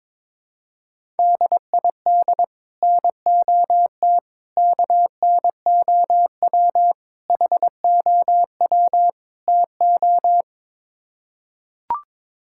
Morse code, heard as a keyed tone, about 22 wpm, 700 Hz. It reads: DID NOT KNOW HOW TO
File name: practice / 22wpm / Build A Phrase - 5-Word Phrases - 22wpm